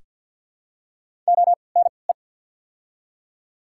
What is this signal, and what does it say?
Morse code, heard as a keyed tone, about 54 wpm, spaced as for 29 wpm, 700 Hz. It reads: ONE
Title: 3x Character Spacing - Top 200 Words - 50wpm